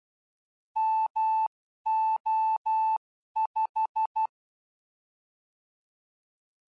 Morse code, heard as a keyed tone, about 12 wpm, 880 Hz. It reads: MO5
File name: ardf.quickstart / mo5